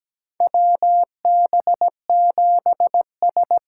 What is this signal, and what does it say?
Morse code, heard as a keyed tone, about 17 wpm, 700 Hz. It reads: WB7S
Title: USA Call Signs - 6 Characters - 6x Character Spacing - 17wpm